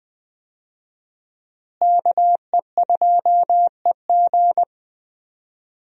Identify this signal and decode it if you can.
Morse code, heard as a keyed tone, about 20 wpm, 700 Hz. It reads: KE2EG
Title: USA Call Signs - 5 Characters - Repeated 2x - 20wpm